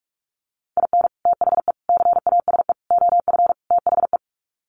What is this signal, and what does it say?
Morse code, heard as a keyed tone, about 44 wpm, 700 Hz. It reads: IN THE CASE OF THE
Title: Sets of 5 Words 45wpm